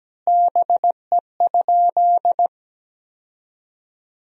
Morse code, heard as a keyed tone, about 17 wpm, 700 Hz. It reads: BE?
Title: 4x Word-Spacing - Sentences from Top 100 Words - 17wpm